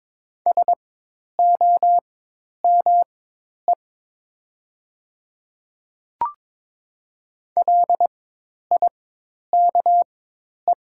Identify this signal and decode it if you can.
Morse code, heard as a keyed tone, about 22 wpm, spaced as for 10 wpm, 700 Hz. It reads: SOME LIKE
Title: 4x Character Spacing - Top 300 Words - 22wpm